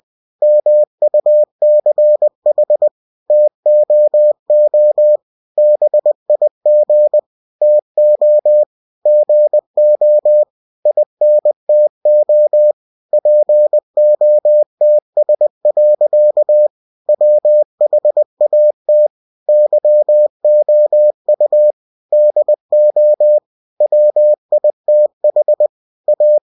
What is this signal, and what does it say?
Morse code, heard as a keyed tone, about 20 wpm, 600 Hz. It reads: MUCH TOO BIG TO GO INTO POTS. WHAT YOU DO WITH A
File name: cw-books / winnie-the-pooh / winnie-20wpm-page-0058